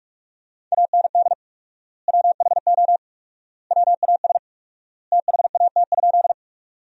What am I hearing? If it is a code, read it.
Morse code, heard as a keyed tone, about 46 wpm, 700 Hz. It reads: AND WHO WAS THAT?
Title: 4x Word-Spacing - Sentences from Top 100 Words - 45wpm